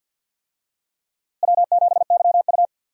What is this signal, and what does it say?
Morse code, heard as a keyed tone, about 47 wpm, 700 Hz. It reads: W7YU
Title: USA Call Signs - 4 Characters - Repeated 3x - 50wpm